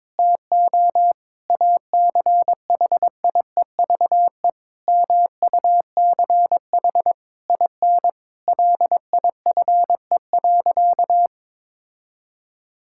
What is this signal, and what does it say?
Morse code, heard as a keyed tone, about 22 wpm, 700 Hz. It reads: TO ACHIEVE MUCH IN LIFE.
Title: Sentences from Top 1000 Words - Review Part 6 - 22wpm